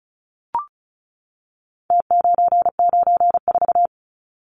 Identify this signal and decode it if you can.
Morse code, heard as a keyed tone, about 34 wpm, 700 Hz. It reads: T994
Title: Numbers - Contest Style - 4-Digit 35wpm